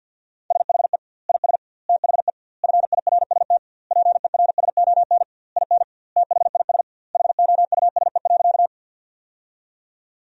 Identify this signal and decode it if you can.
Morse code, heard as a keyed tone, about 50 wpm, 700 Hz. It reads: SHE IS THE FIRST PERSON IN THIS HOUSE.
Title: 2x Word-Spacing - Sentences from Top 400 Words - 50wpm